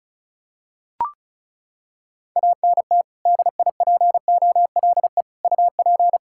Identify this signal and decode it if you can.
Morse code, heard as a keyed tone, about 36 wpm, 700 Hz. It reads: ANT DIPOLE UP
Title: Practice QSO - Exchange 2 - Info - 35wpm